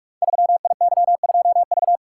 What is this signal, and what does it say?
Morse code, heard as a keyed tone, about 44 wpm, 700 Hz. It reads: 3IY2V
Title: Code Groups - 5 Letter:Numbers - 45wpm